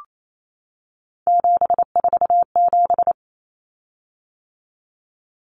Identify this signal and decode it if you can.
Morse code, heard as a keyed tone, about 28 wpm, 700 Hz. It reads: 747
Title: Common Cut Numbers - 3-Digit 28wpm